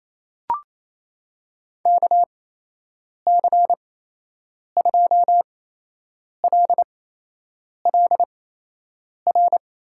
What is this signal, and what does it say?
Morse code, heard as a keyed tone, about 28 wpm, spaced as for 8 wpm, 700 Hz. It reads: KC2LLR
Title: USA Call Signs - 6 Characters - 8x Character Spacing - 28wpm